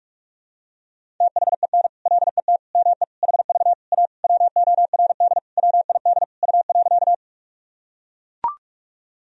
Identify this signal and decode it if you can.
Morse code, heard as a keyed tone, about 41 wpm, 700 Hz. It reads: THEN LET ME HV A WORD WID U.
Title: Sentences from Top 300 Words and Abbreviations 45wpm